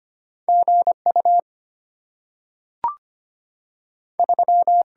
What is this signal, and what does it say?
Morse code, heard as a keyed tone, about 25 wpm, 700 Hz. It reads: G U 3
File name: Two Letter-Number Combinations - 25wpm